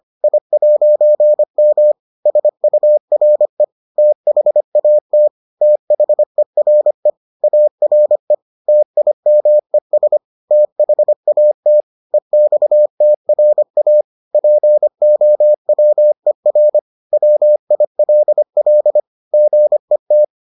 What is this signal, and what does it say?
Morse code, heard as a keyed tone, about 25 wpm, 600 Hz. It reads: I'M SURE THAT THERE ARE TIMES THAT EXTRA POWER WILL GET